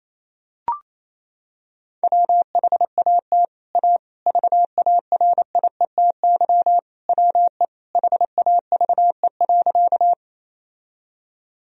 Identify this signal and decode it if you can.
Morse code, heard as a keyed tone, about 28 wpm, 700 Hz. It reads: WHAT A VARIETY WE HAVE.
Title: Sentences from Top 1000 Words - Review Part 4 - 28wpm